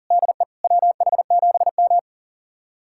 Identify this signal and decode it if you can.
Morse code, heard as a keyed tone, about 40 wpm, 700 Hz. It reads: DE WH7M
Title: USA Callsign DE Callsign - 4 Characters - 40wpm